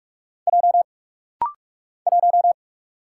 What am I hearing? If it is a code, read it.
Morse code, heard as a keyed tone, about 44 wpm, 700 Hz. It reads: J 1
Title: Single Letter-Number - Rapid-Fire 45wpm